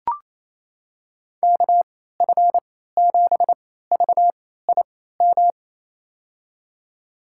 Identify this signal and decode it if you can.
Morse code, heard as a keyed tone, about 28 wpm, spaced as for 16 wpm, 700 Hz. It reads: KF7VIM
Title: USA Call Signs - 6 Characters - 3x Character Spacing - 28wpm